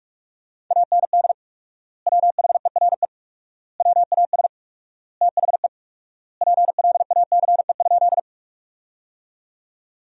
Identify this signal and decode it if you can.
Morse code, heard as a keyed tone, about 46 wpm, 700 Hz. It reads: AND WHERE WAS THE PLACE?
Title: 4x Word-Spacing - Sentences from Top 200 Words - 45wpm